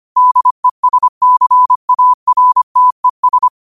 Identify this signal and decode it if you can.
Morse code, heard as a keyed tone, about 25 wpm, 1000 Hz. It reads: DESCARTES